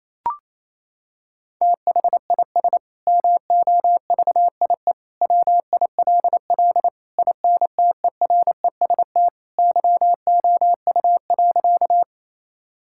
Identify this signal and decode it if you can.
Morse code, heard as a keyed tone, about 28 wpm, 700 Hz. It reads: THIS MOVIE WILL INTEREST YOU.
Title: Sentences from Top 1000 Words - Review Part 5 - 28wpm